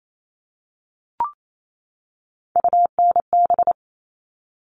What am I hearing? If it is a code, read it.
Morse code, heard as a keyed tone, about 28 wpm, 700 Hz. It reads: UNB
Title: All Cut Numbers - 3-Digit 28wpm